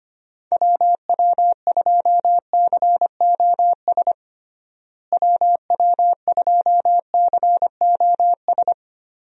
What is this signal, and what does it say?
Morse code, heard as a keyed tone, about 25 wpm, 700 Hz. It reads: WW2COS WW2COS
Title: USA Call Signs - 6 Characters - Repeated 3x - 25wpm